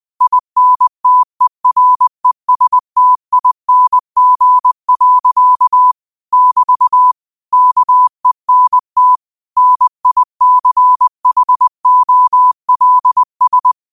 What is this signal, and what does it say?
Morse code, heard as a keyed tone, about 20 wpm, 1000 Hz. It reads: INTERESTING. <BT> KENT NICHOLS
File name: qotdfiles / 20wpm